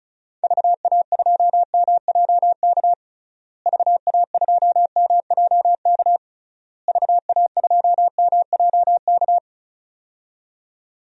Morse code, heard as a keyed tone, about 34 wpm, 700 Hz. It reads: VA2MJK VA2MJK VA2MJK
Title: DXCC-100 Call Signs - 6 Characters - Repeated 3x - 35wpm